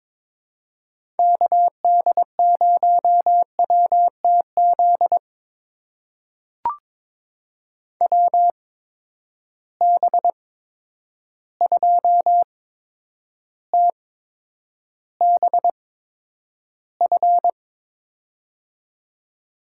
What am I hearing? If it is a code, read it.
Morse code, heard as a keyed tone, about 22 wpm, spaced as for 9 wpm, 700 Hz. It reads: KD0WTZ WB2TBF
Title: USA Call Signs - 6 Characters - 8x Character Spacing - 22wpm